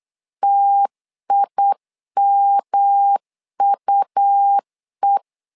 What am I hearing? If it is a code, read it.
Morse code, heard as a keyed tone, about 8 wpm, 800 Hz. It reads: TIMUE
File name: LEZIONI / 3 MWRO / 3 M 8wpm 2300